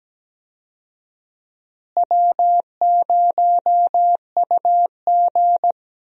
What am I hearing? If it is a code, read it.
Morse code, heard as a keyed tone, about 17 wpm, 700 Hz. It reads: W0UG
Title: USA Call Signs - 4 Characters - Repeated 3x - 17wpm